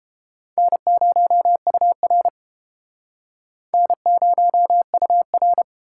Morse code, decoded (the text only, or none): N0UR N0UR